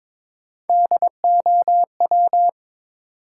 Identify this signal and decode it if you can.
Morse code, heard as a keyed tone, about 22 wpm, 700 Hz. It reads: DOW
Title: Sets of 5 Words - New Word Drill - 22wpm